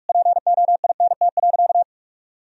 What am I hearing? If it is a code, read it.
Morse code, heard as a keyed tone, about 46 wpm, 700 Hz. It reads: POINT.
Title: Sentences from Top 1000 Words - Review Part 1 - 45wpm